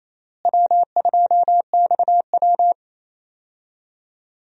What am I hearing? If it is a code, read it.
Morse code, heard as a keyed tone, about 28 wpm, 700 Hz. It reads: W2XW
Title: USA Call Signs - 4 Characters - Repeated 3x - 28wpm